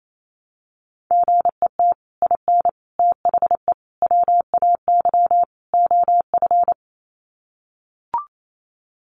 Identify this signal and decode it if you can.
Morse code, heard as a keyed tone, about 28 wpm, 700 Hz. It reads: GET IN THE WAY OF
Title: Sets of 5 Words Encore 28wpm